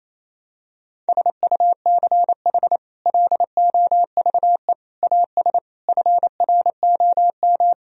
Morse code, heard as a keyed tone, about 28 wpm, 700 Hz. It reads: SUCH LOVE AS FROM
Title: Sentences from Top 1000 Words - Review Part 1 - 28wpm